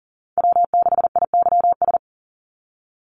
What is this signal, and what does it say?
Morse code, heard as a keyed tone, about 40 wpm, 700 Hz. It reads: W6IYS
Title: NA SOTA Chaser Callsigns 40wpm